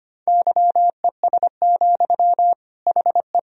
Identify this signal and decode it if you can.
Morse code, heard as a keyed tone, about 25 wpm, 700 Hz. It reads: YES, HE
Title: Sentences from Top 1000 Words - Review Part 4 - 25wpm